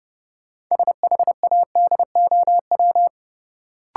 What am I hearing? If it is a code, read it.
Morse code, heard as a keyed tone, about 30 wpm, 700 Hz. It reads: SHADOW
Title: Sets of 5 Words Encore - New Word Drill - 30wpm